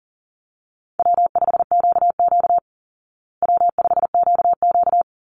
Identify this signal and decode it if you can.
Morse code, heard as a keyed tone, about 40 wpm, 700 Hz. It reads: W5QQ W5QQ
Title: Easy Call Signs Repeated 3x - 40wpm